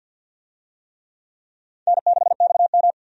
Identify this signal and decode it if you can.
Morse code, heard as a keyed tone, about 45 wpm, 700 Hz. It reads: N6XM
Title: USA Call Signs - 4 Characters - Repeated 2x - 50wpm